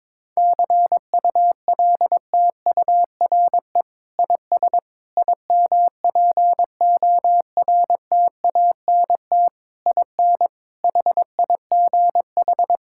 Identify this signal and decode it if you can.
Morse code, heard as a keyed tone, about 22 wpm, 700 Hz. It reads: CULTURE IS IMPORTANT IN HIGH
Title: Sentences from Top 1000 Words - Review Part 6 - 22wpm